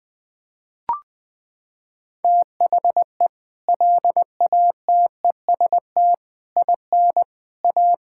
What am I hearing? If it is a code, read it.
Morse code, heard as a keyed tone, about 20 wpm, 700 Hz. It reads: THE LATEST IN A
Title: Sets of 6 Words - 20wpm